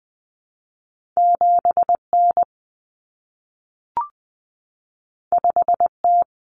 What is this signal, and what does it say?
Morse code, heard as a keyed tone, about 20 wpm, 700 Hz. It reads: 7N 5T
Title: Common Cut Numbers - 2-Digit 20wpm